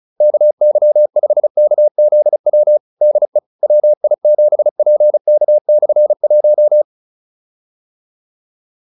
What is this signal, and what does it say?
Morse code, heard as a keyed tone, about 34 wpm, 600 Hz. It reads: KY5KZW DE WI7PK/1